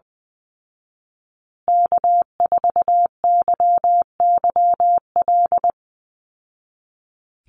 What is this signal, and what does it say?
Morse code, heard as a keyed tone, about 20 wpm, 700 Hz. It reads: K4YYL
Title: Call Signs Repeated 2x - 20wpm